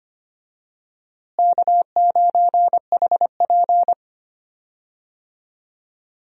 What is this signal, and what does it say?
Morse code, heard as a keyed tone, about 25 wpm, 700 Hz. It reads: K9HP
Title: USA Call Signs - 4 Characters - Repeated 2x - 25wpm